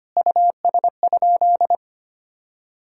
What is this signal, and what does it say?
Morse code, heard as a keyed tone, about 25 wpm, 700 Hz. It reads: US?
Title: Sentences from Top 1000 Words - Review Part 5 - 25wpm